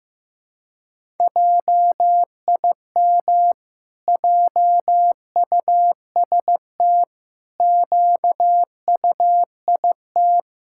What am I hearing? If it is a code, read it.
Morse code, heard as a keyed tone, about 15 wpm, 700 Hz. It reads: JIM JUST QUIT
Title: Pangram Challenge v2 - 15wpm